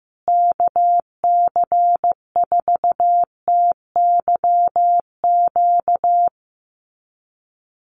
Morse code, heard as a keyed tone, about 15 wpm, 700 Hz. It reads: KC4TYQ
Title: USA Call Signs Repeated 3x 15wpm